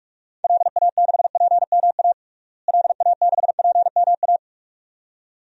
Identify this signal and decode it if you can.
Morse code, heard as a keyed tone, about 46 wpm, 700 Hz. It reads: LA6PMA LA6PMA
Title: DXCC-100 Call Signs - 6 Characters - Repeated 2x - 45wpm